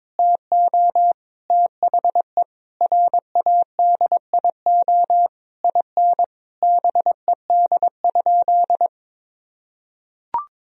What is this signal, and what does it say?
Morse code, heard as a keyed tone, about 22 wpm, 700 Hz. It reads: TO THE RADIO IN BED?
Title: Sentences from Top 1000 Words - Review Part 3 - 22wpm